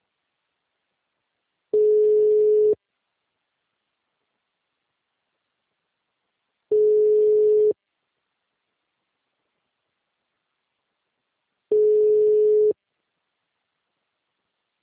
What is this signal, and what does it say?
Ringback tone (Europe, 425 Hz).
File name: out- -2018- - - .73